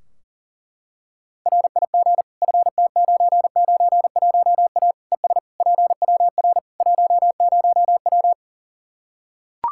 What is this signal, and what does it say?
Morse code, heard as a keyed tone, about 40 wpm, 700 Hz. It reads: RIG FT991A ES PWR 10W
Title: K1USN SST QSO Practice 40wpm